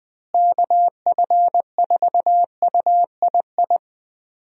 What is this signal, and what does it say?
Morse code, heard as a keyed tone, about 20 wpm, 700 Hz. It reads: KF4UII
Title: USA Call Signs - 6 Characters - Repeated 2x - 20wpm